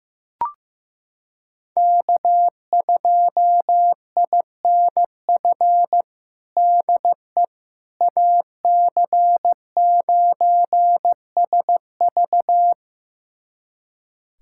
Morse code, heard as a keyed tone, about 16 wpm, 700 Hz. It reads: K2INF DE AC9SV
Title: USA Callsign DE Callsign - 5 Characters - 15wpm